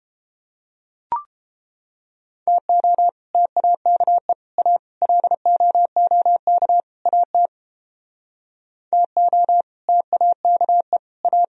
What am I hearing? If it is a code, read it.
Morse code, heard as a keyed tone, about 31 wpm, 700 Hz. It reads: TO TAKE A LOOK AT TO TAKE A